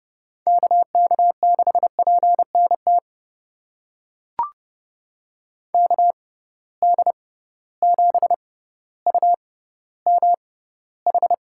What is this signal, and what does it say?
Morse code, heard as a keyed tone, about 30 wpm, spaced as for 15 wpm, 700 Hz. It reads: KK6PNT KD7UMH
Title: USA Call Signs - 6 Characters - 6x Character Spacing - 30wpm